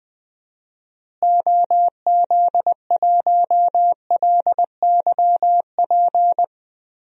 Morse code, heard as a keyed tone, about 20 wpm, 700 Hz. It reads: OZ1LYP
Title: DXCC-100 Call Signs - 6 Characters - Repeated 3x - 20wpm